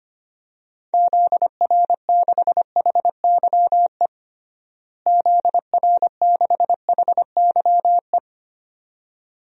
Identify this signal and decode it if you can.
Morse code, heard as a keyed tone, about 25 wpm, 700 Hz. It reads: ZR6HYE ZR6HYE
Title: DXCC-100 Call Signs - 6 Characters - Repeated 2x - 25wpm